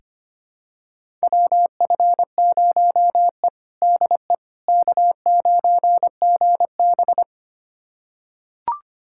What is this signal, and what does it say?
Morse code, heard as a keyed tone, about 25 wpm, 700 Hz. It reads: WF0E DE K9GB
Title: USA Callsign DE Callsign - 4 Characters - 25wpm